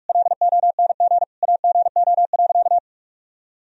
Morse code, heard as a keyed tone, about 41 wpm, 700 Hz. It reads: LONG AGO.
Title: Sentences from Top 1000 Words - Review Part 3 - 45wpm